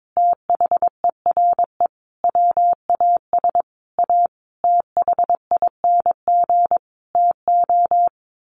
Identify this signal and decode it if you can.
Morse code, heard as a keyed tone, about 22 wpm, 700 Hz. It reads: THERE WAS A THING TO
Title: A Sound Of Thunder 22wpm